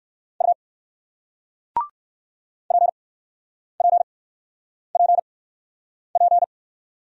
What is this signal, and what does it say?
Morse code, heard as a keyed tone, about 61 wpm, spaced as for 9 wpm, 700 Hz. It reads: U PPPP